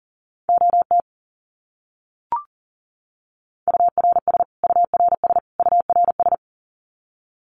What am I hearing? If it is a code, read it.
Morse code, heard as a keyed tone, about 40 wpm, 700 Hz. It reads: OT URS URS URS
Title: Abbreviations Repeated 3x - 40wpm